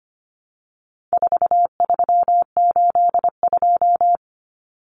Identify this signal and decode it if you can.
Morse code, heard as a keyed tone, about 25 wpm, 700 Hz. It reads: 4382
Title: Common Cut Numbers - 4-Digit 25wpm